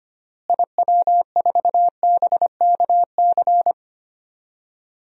Morse code, heard as a keyed tone, about 25 wpm, 700 Hz. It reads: IW4BKC
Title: DXCC-100 Call Signs - 6 Characters - Repeated 2x - 25wpm